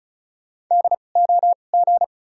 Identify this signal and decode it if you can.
Morse code, heard as a keyed tone, about 34 wpm, 700 Hz. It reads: D O G
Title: 2x Character Spacing - Top 500 Words - 35wpm